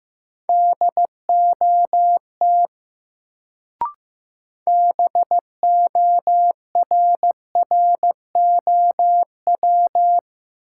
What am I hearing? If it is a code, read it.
Morse code, heard as a keyed tone, about 15 wpm, 700 Hz. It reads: DOT BORROW
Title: Sentences from Top 1500 Words - New Word Drill - 15wpm